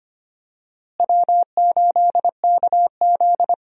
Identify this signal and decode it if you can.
Morse code, heard as a keyed tone, about 25 wpm, 700 Hz. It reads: W8KZ
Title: USA Call Signs - 4 Characters - Repeated 2x - 25wpm